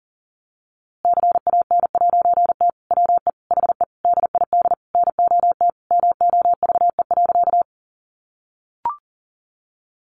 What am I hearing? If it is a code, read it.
Morse code, heard as a keyed tone, about 40 wpm, 700 Hz. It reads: CAN'T WE HE DID NOT MOVE.
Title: A Sound Of Thunder 40wpm